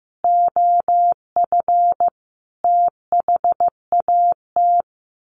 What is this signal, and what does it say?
Morse code, heard as a keyed tone, about 15 wpm, 700 Hz. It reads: OF THAT
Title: Sets of 5 Words Encore 15wpm